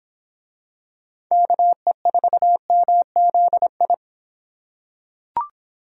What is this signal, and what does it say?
Morse code, heard as a keyed tone, about 25 wpm, 700 Hz. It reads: KE4MZI